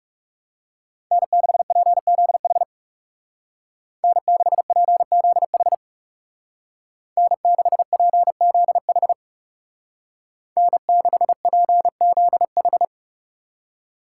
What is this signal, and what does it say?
Morse code, heard as a keyed tone, about 35 wpm, 700 Hz. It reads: N6PZH N6PZH N6PZH N6PZH
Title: Speed Racing - USA 1x3 Call Signs 30wpm